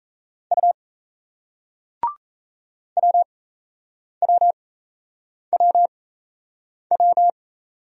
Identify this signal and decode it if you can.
Morse code, heard as a keyed tone, about 38 wpm, spaced as for 8 wpm, 700 Hz. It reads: U WWWW